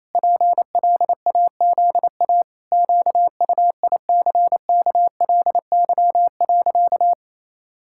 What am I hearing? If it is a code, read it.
Morse code, heard as a keyed tone, about 28 wpm, 700 Hz. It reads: PLAZA QUICKLY.
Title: Pangram Challenge v2 - 28wpm